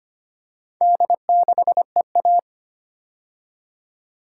Morse code, heard as a keyed tone, about 25 wpm, 700 Hz. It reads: D6EA
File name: practice / 25wpm / Four Letter-Number Combinations - 25wpm